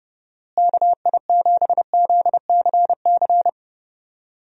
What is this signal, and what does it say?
Morse code, heard as a keyed tone, about 30 wpm, 700 Hz. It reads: KI7ZCC
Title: USA Call Signs - 6 Characters - Repeated 2x - 30wpm